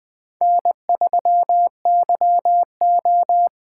Morse code, heard as a keyed tone, about 21 wpm, 700 Hz. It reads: N3YO